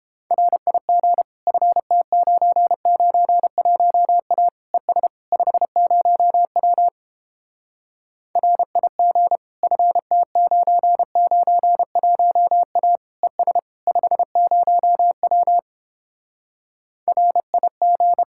Morse code, heard as a keyed tone, about 29 wpm, 700 Hz. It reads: RIG FT991A ES 50W RIG FT991A ES 50W RIG